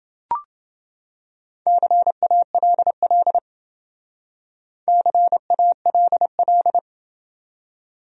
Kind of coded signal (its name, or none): Morse code